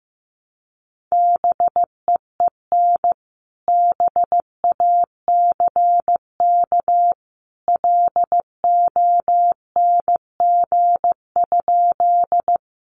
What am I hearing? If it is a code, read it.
Morse code, heard as a keyed tone, about 15 wpm, 700 Hz. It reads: BEEN BACK LONG?
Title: Sentences from Top 200 Words 15wpm